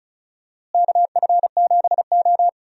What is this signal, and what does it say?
Morse code, heard as a keyed tone, about 36 wpm, 700 Hz. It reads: KF7O
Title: USA Call Signs - 4 Characters - Repeated 3x - 35wpm